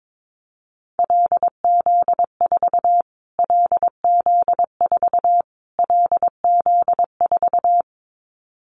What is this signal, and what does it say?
Morse code, heard as a keyed tone, about 22 wpm, 700 Hz. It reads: LZ4 LZ4 LZ4
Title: Callsign Prefixes for Top 25 DXCC - Repeated 3x - 22wpm